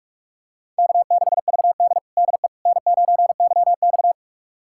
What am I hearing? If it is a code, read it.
Morse code, heard as a keyed tone, about 44 wpm, 690 Hz. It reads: K6VD DE N9YX
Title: USA Callsign DE Callsign - 4 Characters - 45wpm